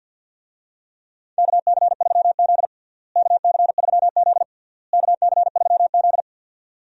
Morse code, heard as a keyed tone, about 45 wpm, 700 Hz. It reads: KC3Z KC3Z KC3Z
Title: USA Call Signs - 4 Characters - Repeated 3x - 50wpm